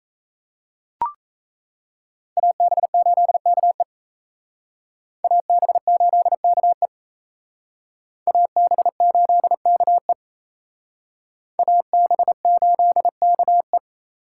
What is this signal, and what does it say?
Morse code, heard as a keyed tone, about 36 wpm, 700 Hz. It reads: AB8KE AB8KE AB8KE AB8KE